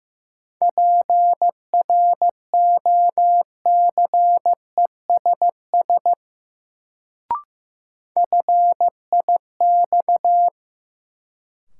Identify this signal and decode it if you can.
Morse code, heard as a keyed tone, about 15 wpm, 700 Hz. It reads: PROCESS FIX
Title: Sentences from Top 1400 Words - New Word Drill Alt - 15wpm